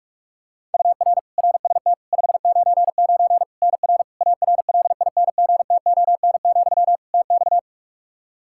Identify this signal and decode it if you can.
Morse code, heard as a keyed tone, about 41 wpm, 700 Hz. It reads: UR RST 599 NR ARLINGTON, TX
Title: Practice QSO - Exchange 1 - Beginning - 45wpm